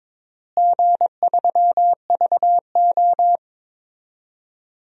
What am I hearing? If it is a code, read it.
Morse code, heard as a keyed tone, about 22 wpm, 700 Hz. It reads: G3VO
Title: DXCC-100 Call Signs - 4 Characters - Repeated 2x - 22wpm